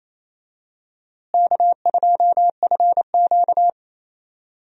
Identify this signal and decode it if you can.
Morse code, heard as a keyed tone, about 28 wpm, 700 Hz. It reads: K2FQ
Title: USA Call Signs - 4 Characters - Repeated 3x - 28wpm